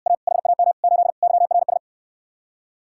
Morse code, heard as a keyed tone, about 67 wpm, 700 Hz. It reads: A HAM 8 YRS